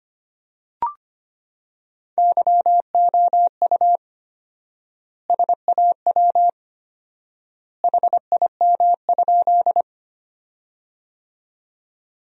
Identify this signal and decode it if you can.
Morse code, heard as a keyed tone, about 25 wpm, 700 Hz. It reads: YOU SAW HIM?
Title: 4x Word-Spacing - Sentences from Top 300 Words - 25wpm